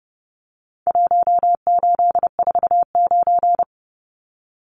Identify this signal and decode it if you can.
Morse code, heard as a keyed tone, about 30 wpm, 700 Hz. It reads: 1849
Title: Numbers - Contest Style - 4-Digit 30wpm